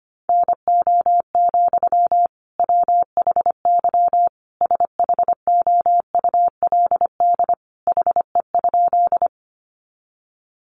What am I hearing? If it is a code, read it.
Morse code, heard as a keyed tone, about 25 wpm, 700 Hz. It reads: NO, WHY SHOULD HE?
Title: Sentences from Top 300 Words 25wpm